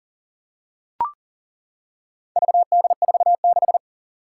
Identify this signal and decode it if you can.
Morse code, heard as a keyed tone, about 40 wpm, 700 Hz. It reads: VD46
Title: Four Challenging Letter-Number Combinations 40wpm